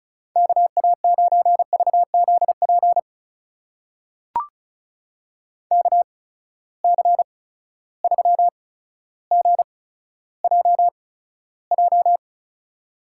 Morse code, heard as a keyed tone, about 36 wpm, spaced as for 15 wpm, 700 Hz. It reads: KA9VZP KC3GJJ
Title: USA Call Signs - 6 Characters - 8x Character Spacing - 35wpm